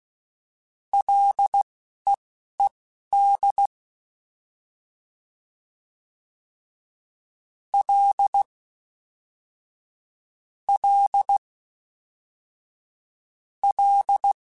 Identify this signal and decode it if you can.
Morse code, heard as a keyed tone, about 16 wpm, spaced as for 11 wpm, 800 Hz. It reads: LEED L L L
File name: F and L / F and L mp3 / F and L Prefixes 3x2 Twice @10